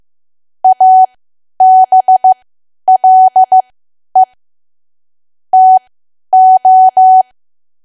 Morse code, heard as a keyed tone, about 15 wpm, spaced as for 10 wpm, 750 Hz. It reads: ABLE TO
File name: cwa / W1AW-Level2 / 10WPM